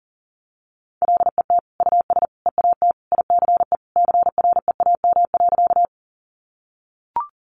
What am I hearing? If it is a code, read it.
Morse code, heard as a keyed tone, about 40 wpm, 700 Hz. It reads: LET US EAT ICE CREAM.
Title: Sentences from Top 1500 Words 40wpm